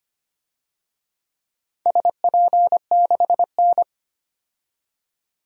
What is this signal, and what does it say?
Morse code, heard as a keyed tone, about 25 wpm, 700 Hz. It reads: SP6N
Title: DXCC-100 Call Signs - 4 Characters - 25wpm